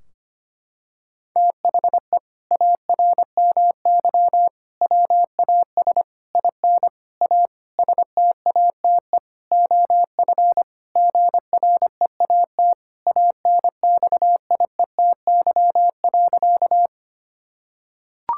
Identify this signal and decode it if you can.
Morse code, heard as a keyed tone, about 25 wpm, 700 Hz. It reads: THE ARMY WAS IN A STATE OF GREAT ANXIETY.
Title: Sentences from Top 1600 Words 25wpm